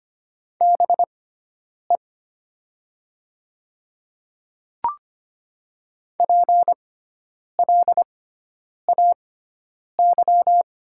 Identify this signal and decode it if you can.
Morse code, heard as a keyed tone, about 25 wpm, spaced as for 9 wpm, 700 Hz. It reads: BE PLAY
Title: 6x Character Spacing - Top 300 Words - 25wpm